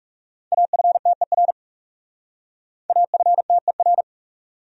Morse code, heard as a keyed tone, about 38 wpm, 700 Hz. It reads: AFTER AFTER